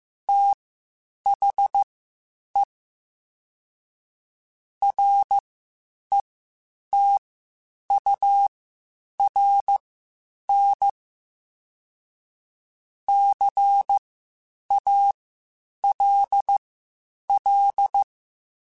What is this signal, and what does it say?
Morse code, heard as a keyed tone, about 15 wpm, spaced as for 8 wpm, 800 Hz. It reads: THE RETURN CALL
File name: (spaced) / QSOs Spaced-mp3s / kc7haa @13 sp